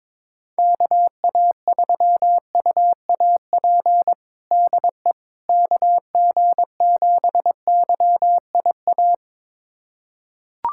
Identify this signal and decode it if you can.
Morse code, heard as a keyed tone, about 22 wpm, 700 Hz. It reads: KA3UAP DE KG7YIA
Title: USA Callsign DE Callsign - 6 Characters - 22wpm